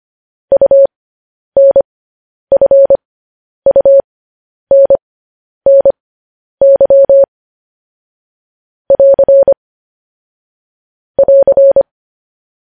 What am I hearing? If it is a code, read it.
Morse code, heard as a keyed tone, about 25 wpm, spaced as for 10 wpm, 550 Hz. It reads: UNFUNNY <AR> <AR>
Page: news headlines in morse code at 10 wpm